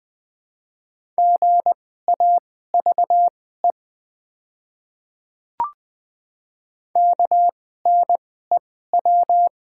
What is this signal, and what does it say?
Morse code, heard as a keyed tone, about 20 wpm, spaced as for 14 wpm, 700 Hz. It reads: GAVE KNEW
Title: 2x Character Spacing - Top 500 Words - 20wpm